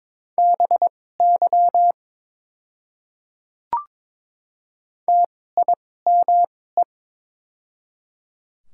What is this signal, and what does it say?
Morse code, heard as a keyed tone, about 22 wpm, spaced as for 16 wpm, 700 Hz. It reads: BY TIME
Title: 2x Character Spacing - Top 100 Words - 22wpm